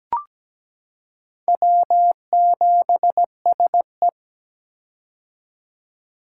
Morse code, heard as a keyed tone, about 17 wpm, 700 Hz. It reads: W7SE
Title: K1USN SST QSO Practice - 17wpm